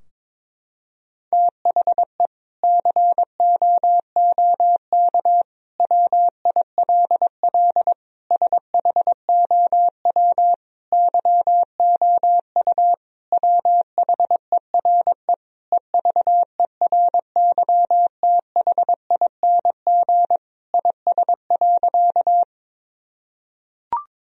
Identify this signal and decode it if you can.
Morse code, heard as a keyed tone, about 22 wpm, 700 Hz. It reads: THE COOK WILL SHOW YOU WHERE EVERYTHING IS.
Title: Sentences from Top 1000 Words - Review Part 4 - 22wpm